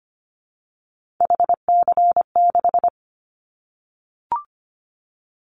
Morse code, heard as a keyed tone, about 25 wpm, 700 Hz. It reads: HC6
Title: Callsign Prefixes for Top 100 DXCC - 25wpm